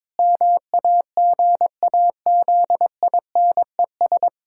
Morse code, heard as a keyed tone, about 22 wpm, 700 Hz. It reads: MAGAZINES
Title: Pangram Challenge v2 - 22wpm